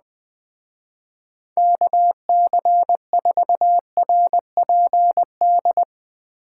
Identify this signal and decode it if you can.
Morse code, heard as a keyed tone, about 20 wpm, 700 Hz. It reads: KC4RPD
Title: Speed Racing - USA 2x3 Call Signs 17wpm